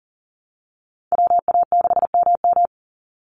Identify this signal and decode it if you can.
Morse code, heard as a keyed tone, about 40 wpm, 700 Hz. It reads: WA6MM
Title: NA SOTA Activator Callsigns 40wpm